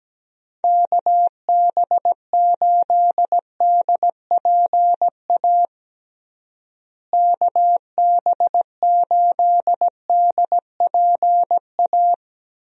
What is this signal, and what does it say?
Morse code, heard as a keyed tone, about 17 wpm, 700 Hz. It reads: KB8DPA KB8DPA
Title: USA Call Signs - 6 Characters - Repeated 2x - 17wpm